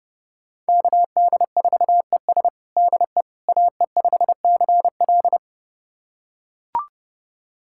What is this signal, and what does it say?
Morse code, heard as a keyed tone, about 30 wpm, 700 Hz. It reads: KD4ES DE AE5CL
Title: USA Callsign DE Callsign - 5 Characters - 30wpm